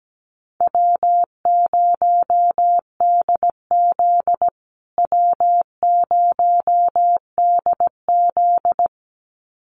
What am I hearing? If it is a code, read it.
Morse code, heard as a keyed tone, about 17 wpm, 700 Hz. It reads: W0DZ W0DZ
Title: USA Call Signs Repeated 2x 17wpm